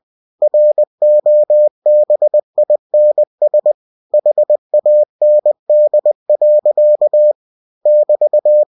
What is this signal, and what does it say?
Morse code, heard as a keyed tone, about 20 wpm, 600 Hz. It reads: ROBINS HAND. <BT>
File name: cw-books / winnie-the-pooh / winnie-20wpm-page-0043